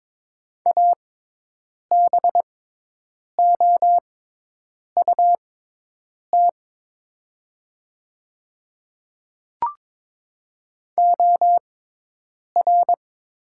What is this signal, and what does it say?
Morse code, heard as a keyed tone, about 22 wpm, spaced as for 8 wpm, 700 Hz. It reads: ABOUT OR
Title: 6x Character Spacing - Top 100 Words - 22wpm